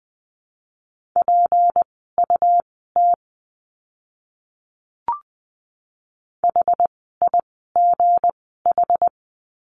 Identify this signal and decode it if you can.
Morse code, heard as a keyed tone, about 20 wpm, spaced as for 14 wpm, 700 Hz. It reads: PUT HIGH
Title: 237 Lesson 38f - Letter q - Words 20wpm